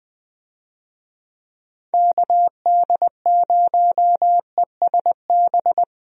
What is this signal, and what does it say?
Morse code, heard as a keyed tone, about 21 wpm, 700 Hz. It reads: KD0ESB
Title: USA Call Signs - 6 Characters - 20wpm